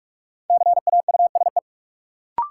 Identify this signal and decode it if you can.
Morse code, heard as a keyed tone, about 42 wpm, 700 Hz. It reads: CAUSE
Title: Sentences from Top 200 Words - New Word Drill - 45wpm